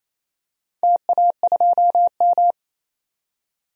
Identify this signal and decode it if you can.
Morse code, heard as a keyed tone, about 28 wpm, 700 Hz. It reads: TA2M
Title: DXCC-100 Call Signs - 4 Characters - 28wpm